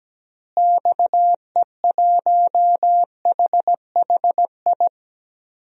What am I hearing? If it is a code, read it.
Morse code, heard as a keyed tone, about 17 wpm, 700 Hz. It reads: XE1HHI
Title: DXCC-100 Call Signs - 6 Characters - Repeated 3x - 17wpm